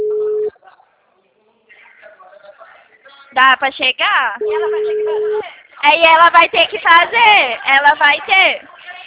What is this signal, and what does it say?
Ringback tone (Europe, 425 Hz), with speech.
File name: out- -2018- - - .56467